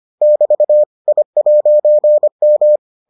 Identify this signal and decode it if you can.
Morse code, heard as a keyed tone, about 25 wpm, 600 Hz. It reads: <BT> I'M